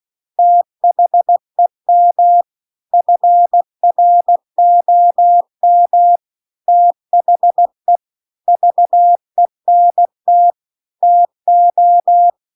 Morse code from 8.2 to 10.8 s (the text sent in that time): VENT